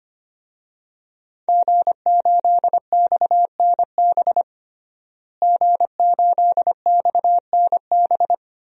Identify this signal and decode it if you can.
Morse code, heard as a keyed tone, about 25 wpm, 700 Hz. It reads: G8XNB G8XNB
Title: DXCC-100 Call Signs - 5 Characters - Repeated 2x - 25wpm